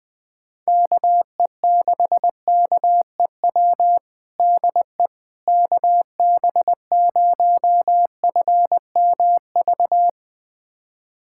Morse code, heard as a keyed tone, about 20 wpm, 700 Hz. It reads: KE6KEW DE KB0FMV
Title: USA Callsign DE Callsign 20wpm